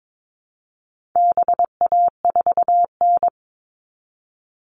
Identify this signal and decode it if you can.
Morse code, heard as a keyed tone, about 22 wpm, 700 Hz. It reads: BA4N
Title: All Cut Numbers - 4-Digit 22wpm